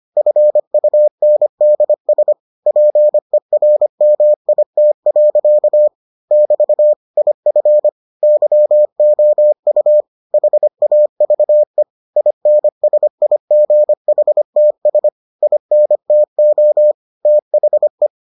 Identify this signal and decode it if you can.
Morse code, heard as a keyed tone, about 25 wpm, 600 Hz. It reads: FUNDS PERMIT. <BT> IF YOU HAVE INSIGHTS INTO THE